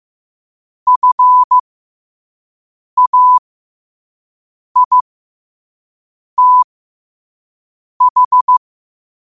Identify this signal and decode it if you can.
Morse code, heard as a keyed tone, about 15 wpm, spaced as for 5 wpm, 1000 Hz. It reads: FAITH
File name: qotdfiles / 5wpm